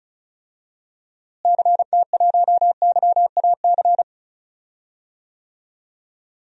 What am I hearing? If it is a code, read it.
Morse code, heard as a keyed tone, about 34 wpm, 700 Hz. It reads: CT1YAC
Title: DXCC-100 Call Signs - 6 Characters - Repeated 2x - 35wpm